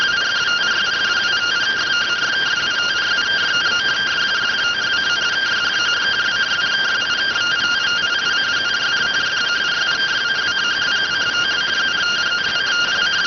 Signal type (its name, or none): two-tone teleprinter signal